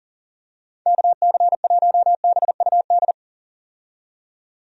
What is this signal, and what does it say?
Morse code, heard as a keyed tone, about 40 wpm, 700 Hz. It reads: KC1BUD
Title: USA Call Signs - 6 Characters - 40wpm